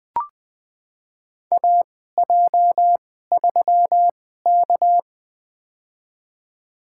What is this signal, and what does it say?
Morse code, heard as a keyed tone, about 20 wpm, spaced as for 15 wpm, 700 Hz. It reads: AJ3K
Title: USA Call Signs - 2x Character Spacing - 20wpm